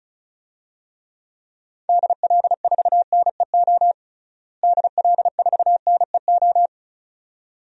Morse code, heard as a keyed tone, about 36 wpm, 700 Hz. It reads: DL4NEO DL4NEO
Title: DXCC-100 Call Signs - 6 Characters - Repeated 2x - 35wpm